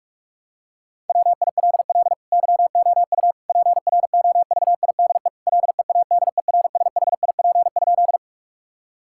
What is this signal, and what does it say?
Morse code, heard as a keyed tone, about 46 wpm, 700 Hz. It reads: WILL YOU PROVIDE LEADERSHIP?
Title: Sentences from Top 1600 Words 45wpm